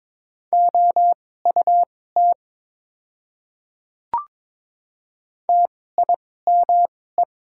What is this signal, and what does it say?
Morse code, heard as a keyed tone, about 22 wpm, spaced as for 16 wpm, 700 Hz. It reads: OUT TIME
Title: 2x Character Spacing - Top 100 Words - 22wpm